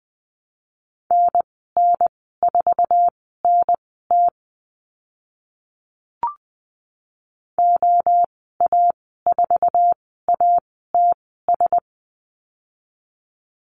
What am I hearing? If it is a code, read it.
Morse code, heard as a keyed tone, about 20 wpm, spaced as for 15 wpm, 700 Hz. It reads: NN4NT OA4ATS
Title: 062 Lesson 11g - Letter h - Call Signs 20wpm